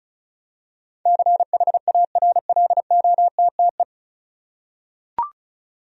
Morse code, heard as a keyed tone, about 34 wpm, 700 Hz. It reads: CHARLOTTE
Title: USA Common Names - 35wpm